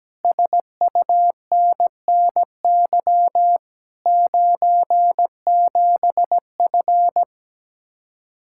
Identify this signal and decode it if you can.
Morse code, heard as a keyed tone, about 17 wpm, 700 Hz. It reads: SUNNY 97F
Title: Speed-Racing Common QSO Phrases - 17wpm